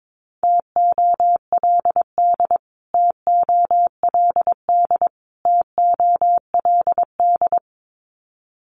Morse code, heard as a keyed tone, about 22 wpm, 700 Hz. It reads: TOLD TOLD TOLD
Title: Top 500 Words Repeated 22wpm